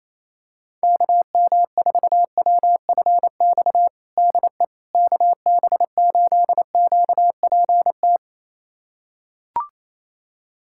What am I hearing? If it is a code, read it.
Morse code, heard as a keyed tone, about 28 wpm, 700 Hz. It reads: KM4WFX DE KB8QPT
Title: USA Callsign DE Callsign - 6 Characters - 28wpm